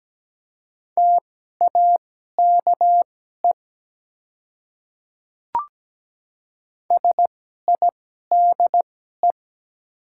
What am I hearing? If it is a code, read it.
Morse code, heard as a keyed tone, about 17 wpm, spaced as for 12 wpm, 700 Hz. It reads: TAKE SIDE
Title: 2x Character Spacing - Top 200 Words - 17wpm